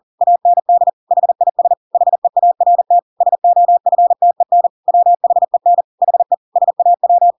Morse code, heard as a keyed tone, about 40 wpm, 700 Hz. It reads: AND HIS HEART SOFTEN WHEN HE SAW
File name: Pinocchio / Pinocchio ch8 40 40 ik1hge